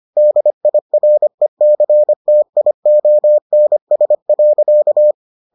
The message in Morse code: DIRECTIONS.

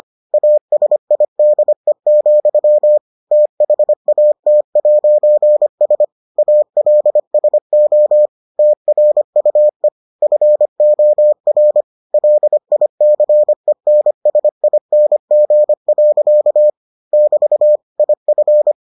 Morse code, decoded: ASIDE, THAT'S ALSO TRUE FOR LICENSING. <BT> IF

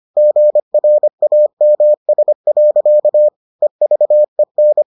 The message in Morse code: GRAMS. EVEN

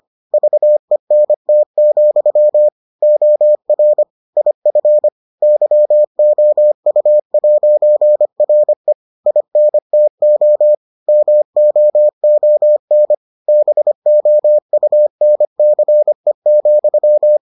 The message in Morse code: VENT, OR IF YOU'RE INTO MOON BOUNCE,

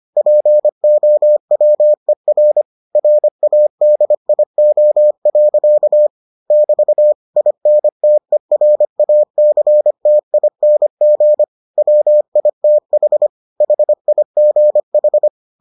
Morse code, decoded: POWER RADIO. <BT> INTERACTING WITH HIGH

